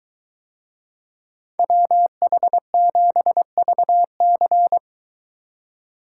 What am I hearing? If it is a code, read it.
Morse code, heard as a keyed tone, about 21 wpm, 700 Hz. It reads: WH7VC